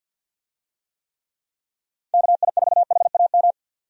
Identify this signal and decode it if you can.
Morse code, heard as a keyed tone, about 48 wpm, 700 Hz. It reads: KI4HAM
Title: USA Call Signs - 6 Characters - 50wpm